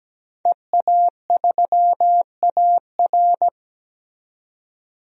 Morse code, heard as a keyed tone, about 17 wpm, 700 Hz. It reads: EA3AR
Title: DXCC-100 Call Signs - 5 Characters - 17wpm